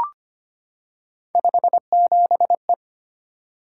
Morse code, heard as a keyed tone, about 25 wpm, 700 Hz. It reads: 57E
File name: Three Letter-Number Combinations - 25wpm